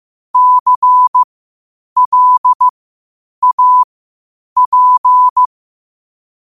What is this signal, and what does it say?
Morse code, heard as a keyed tone, about 15 wpm, spaced as for 8 wpm, 1000 Hz. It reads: CLAP
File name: qotdfiles / 8wpm